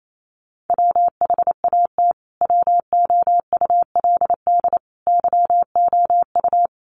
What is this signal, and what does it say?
Morse code, heard as a keyed tone, about 28 wpm, 700 Hz. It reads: WHAT WOULD YOU
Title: Sentences from Top 1200 Words 28wpm